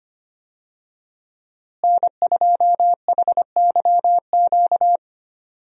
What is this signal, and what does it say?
Morse code, heard as a keyed tone, about 25 wpm, 700 Hz. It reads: N2HYQ
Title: USA Call Signs - 5 Characters - Repeated 2x - 25wpm